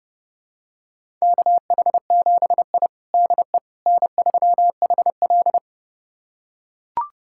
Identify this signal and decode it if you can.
Morse code, heard as a keyed tone, about 31 wpm, 700 Hz. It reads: KH7I DE N3HL
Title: USA Callsign DE Callsign - 4 Characters - 30wpm